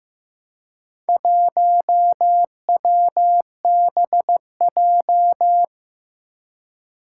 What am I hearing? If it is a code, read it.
Morse code, heard as a keyed tone, about 15 wpm, 700 Hz. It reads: 1WBJ
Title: Four Challenging Letter-Number Combinations 15wpm